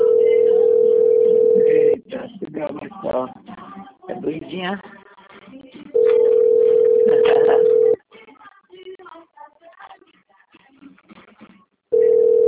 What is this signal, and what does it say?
Ringback tone (North America), 440 + 480 Hz, with speech.
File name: out- - - - - .28649